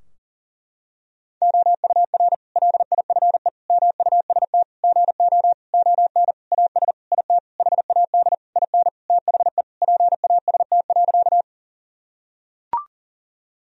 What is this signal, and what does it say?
Morse code, heard as a keyed tone, about 40 wpm, 700 Hz. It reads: OUR LIFE MUST GO ON AS IT HAD IN THE PAST.
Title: Sentences from Top 1000 Words - Review Part 3 - 40wpm